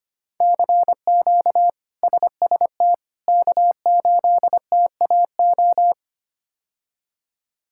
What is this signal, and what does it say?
Morse code, heard as a keyed tone, about 25 wpm, 700 Hz. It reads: CQ SST K8TAO
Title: K1USN SST QSO Practice - 25wpm